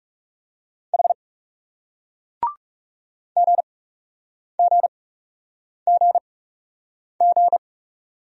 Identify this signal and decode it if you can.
Morse code, heard as a keyed tone, about 39 wpm, spaced as for 8 wpm, 700 Hz. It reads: H GGGG